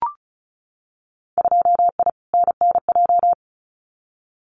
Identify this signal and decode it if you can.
Morse code, heard as a keyed tone, about 34 wpm, 700 Hz. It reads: 2I NNJ
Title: Winter Field Day Exchanges Encore 35wpm